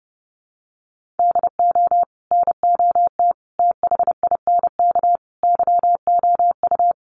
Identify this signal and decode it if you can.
Morse code, heard as a keyed tone, about 30 wpm, 700 Hz. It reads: DO NOT THINK YOU
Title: Sets of 5 Words 30wpm